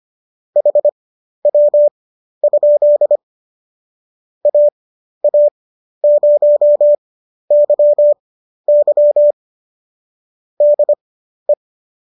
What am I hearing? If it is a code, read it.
Morse code, heard as a keyed tone, about 25 wpm, spaced as for 12 wpm, 590 Hz. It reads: HW? AA0YY DE